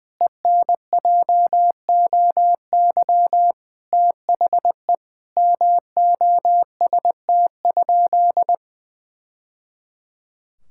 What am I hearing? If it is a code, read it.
Morse code, heard as a keyed tone, about 20 wpm, 700 Hz. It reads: ENJOY THE MOST?
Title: Sentences from Top 1000 Words - Review Part 5 - 20wpm